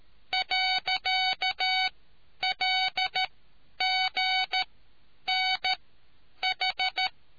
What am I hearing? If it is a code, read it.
Morse code, heard as a keyed tone, about 13 wpm, spaced as for 9 wpm, 750 Hz. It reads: .LGNH